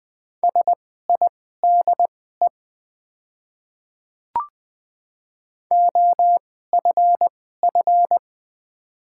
Morse code, heard as a keyed tone, about 20 wpm, spaced as for 15 wpm, 700 Hz. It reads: SIDE OFF
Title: 2x Character Spacing - Top 200 Words - 20wpm